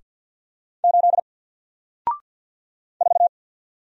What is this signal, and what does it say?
Morse code, heard as a keyed tone, about 48 wpm, 700 Hz. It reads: 8 4
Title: Single Letter-Number Alt 50wpm